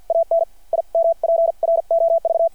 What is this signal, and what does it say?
Morse code, heard as a keyed tone, about 52 wpm, spaced as for 43 wpm, 660 Hz. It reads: AN IMPROV
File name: archived / CRSnet